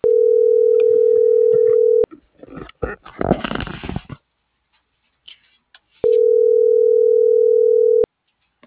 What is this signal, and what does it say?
Ringback tone (North America), 440 + 480 Hz.